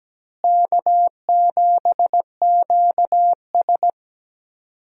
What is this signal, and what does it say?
Morse code, heard as a keyed tone, about 17 wpm, 700 Hz. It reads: K7QS
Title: USA Call Signs - 4 Characters - Repeated 3x - 17wpm